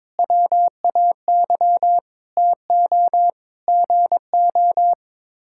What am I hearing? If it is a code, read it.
Morse code, heard as a keyed tone, about 22 wpm, 700 Hz. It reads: WAY TO GO
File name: practice / 22wpm / Sets of 6 Words - 22wpm